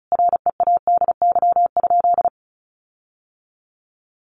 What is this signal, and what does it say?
Morse code, heard as a keyed tone, about 34 wpm, 700 Hz. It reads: READY?
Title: Sentences from Top 1400 Words 35wpm